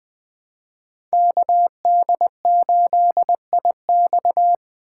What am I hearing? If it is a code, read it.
Morse code, heard as a keyed tone, about 20 wpm, 700 Hz. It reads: KD8IX
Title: USA Call Signs - 5 Characters - 4x Character Spacing - 20wpm